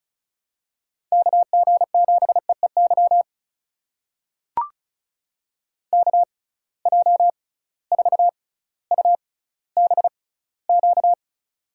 Morse code, heard as a keyed tone, about 36 wpm, spaced as for 18 wpm, 700 Hz. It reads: KG7EEY KJ4UBQ
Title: USA Call Signs - 6 Characters - 6x Character Spacing - 35wpm